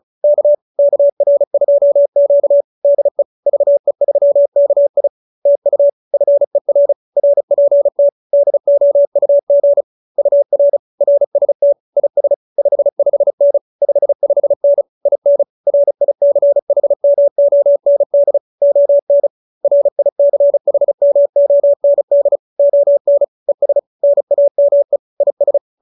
Morse code, heard as a keyed tone, about 34 wpm, 570 Hz. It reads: K KR2Q DE VE3KI TU FER RPT DOUG UR RST IS 55N 55N IN RICHMOND ON RICHMOND ON ES NAME IS